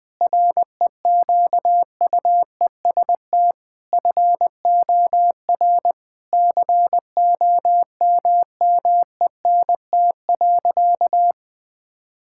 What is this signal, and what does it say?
Morse code, heard as a keyed tone, about 20 wpm, 700 Hz. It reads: REQUEST FOR COMMENT.
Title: Sentences with 8-Word Phrases - Spoken Beginning - 20wpm